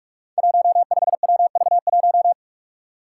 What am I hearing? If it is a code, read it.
Morse code, heard as a keyed tone, about 44 wpm, 700 Hz. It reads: 15WV1
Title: Five Challenging Letter-Number Combinations 45wpm